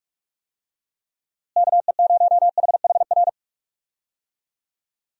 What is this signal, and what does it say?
Morse code, heard as a keyed tone, about 42 wpm, 700 Hz. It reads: KE0HHR
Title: USA Call Signs - 6 Characters - Repeated 3x - 45wpm